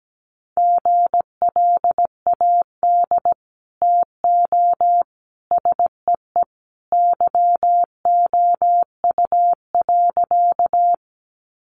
Morse code, heard as a keyed tone, about 17 wpm, 700 Hz. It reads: GLAD TO SEE YOU.
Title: Sentences from Top 800 Words 17wpm